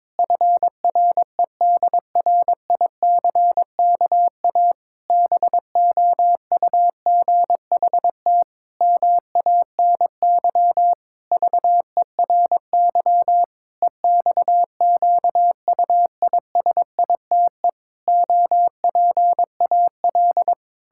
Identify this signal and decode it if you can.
Morse code, heard as a keyed tone, about 22 wpm, 700 Hz. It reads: FREDRICKA BOUGHT MANY VERY EXQUISITE OPAL
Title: Pangram Challenge v2 - 22wpm